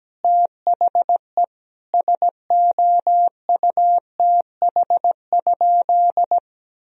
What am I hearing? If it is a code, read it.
Morse code, heard as a keyed tone, about 17 wpm, 700 Hz. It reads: THE SOUTH?
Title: Sentences from Top 1000 Words - Review Part 2 - 17wpm